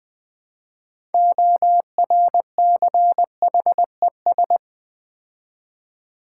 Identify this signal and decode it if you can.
Morse code, heard as a keyed tone, about 20 wpm, 700 Hz. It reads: ORCHES
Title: Build A Word - 3 Letters At A Time - 9-Letter Words - 20wpm